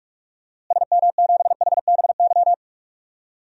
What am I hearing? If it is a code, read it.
Morse code, heard as a keyed tone, about 46 wpm, 700 Hz. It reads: SM7HBY
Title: DXCC-100 Call Signs - 6 Characters - Repeated 3x - 45wpm